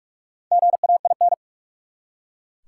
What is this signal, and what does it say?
Morse code, heard as a keyed tone, about 41 wpm, 700 Hz. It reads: GAIN
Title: Four Letter Words - 45wpm